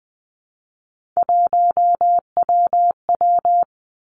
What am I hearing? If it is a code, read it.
Morse code, heard as a keyed tone, about 20 wpm, 700 Hz. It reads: 1WW
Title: Three Challenging Letter-Number Combinations 20wpm